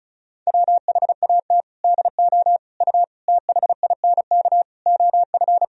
Morse code, heard as a keyed tone, about 34 wpm, 700 Hz. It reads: WHAT DO U THINK OF
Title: Sentences from Top 200 Words and Abbreviations 35wpm